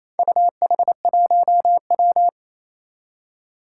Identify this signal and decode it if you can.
Morse code, heard as a keyed tone, about 28 wpm, 700 Hz. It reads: UH1W
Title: Four Challenging Letter-Number Combinations 28wpm